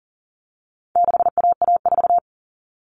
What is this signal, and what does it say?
Morse code, heard as a keyed tone, about 40 wpm, 700 Hz. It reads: 6AA4
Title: All Cut Numbers - 4-Digit 40wpm